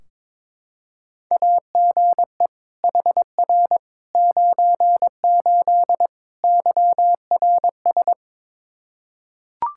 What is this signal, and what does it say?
Morse code, heard as a keyed tone, about 22 wpm, 700 Hz. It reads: AGE HR 98 YRS
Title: K1USN SST QSO Practice 22wpm